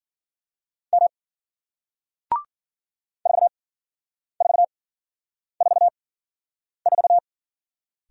Morse code, heard as a keyed tone, about 53 wpm, spaced as for 9 wpm, 700 Hz. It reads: M 4444